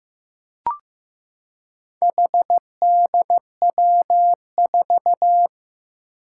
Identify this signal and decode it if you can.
Morse code, heard as a keyed tone, about 15 wpm, 700 Hz. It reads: HDW4
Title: Four Challenging Letter-Number Combinations 15wpm